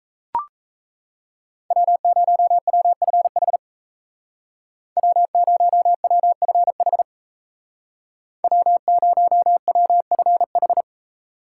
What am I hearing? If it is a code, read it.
Morse code, heard as a keyed tone, about 36 wpm, 700 Hz. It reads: W0WFH W0WFH W0WFH